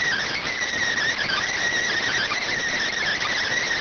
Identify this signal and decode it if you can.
Emergency Alert System (SAME) tones, heard as harsh buzzing bursts. It is heard from 1.2 to 3.6 s.